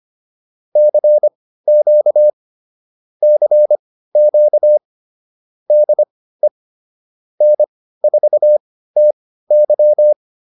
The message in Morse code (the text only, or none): CQ CQ DE N4TY